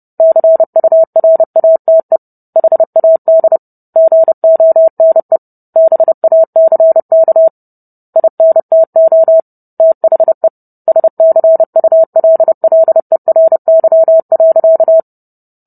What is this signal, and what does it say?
Morse code, heard as a keyed tone, about 30 wpm, 650 Hz. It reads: CURATE HAD GONE BACK INTO THE SCULLERY.